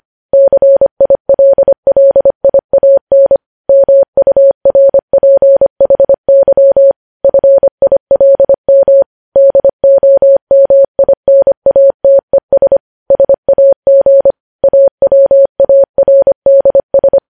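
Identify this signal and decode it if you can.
Morse code, heard as a keyed tone, about 25 wpm, 550 Hz. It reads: CILLIAN MURPHY FILM DOMINATES SAG AWARDS
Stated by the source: Morse code practice of news headlines at 25 wpm, opening with vvv.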